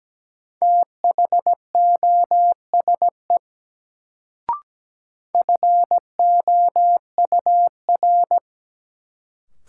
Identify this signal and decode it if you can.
Morse code, heard as a keyed tone, about 17 wpm, 700 Hz. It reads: THOSE FOUR
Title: Sentences from Top 300 Words - New Word Drill - 17wpm